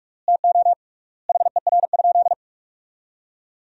Morse code, heard as a keyed tone, about 41 wpm, 700 Hz. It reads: TO HER?
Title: 3x Word-Spacing - Sentences from Top 200 Words - 45wpm